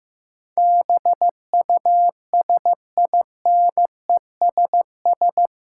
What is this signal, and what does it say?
Morse code, heard as a keyed tone, about 15 wpm, 700 Hz. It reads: BUSINESS
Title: Sentences from Top 600 Words - New Word Drill - 15wpm